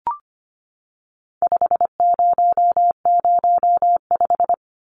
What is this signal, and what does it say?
Morse code, heard as a keyed tone, about 25 wpm, 700 Hz. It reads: 5005
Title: Numbers - 4-Digit 25wpm